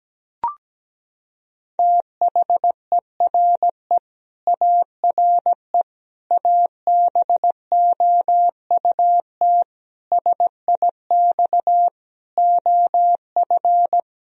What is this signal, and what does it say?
Morse code, heard as a keyed tone, about 17 wpm, 700 Hz. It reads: THERE ARE ABOUT SIX OF
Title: Sentences from Top 1000 Words - Review Part 2 - 17wpm